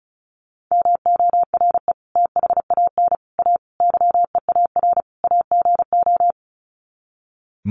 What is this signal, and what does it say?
Morse code, heard as a keyed tone, about 34 wpm, 700 Hz. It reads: MORE THAN A YEAR AGO
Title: Sets of 5 Words Encore2 35wpm